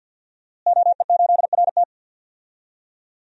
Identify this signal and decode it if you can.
Morse code, heard as a keyed tone, about 47 wpm, 700 Hz. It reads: OE8RT
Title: DXCC-100 Call Signs - 5 Characters - Repeated 3x - 50wpm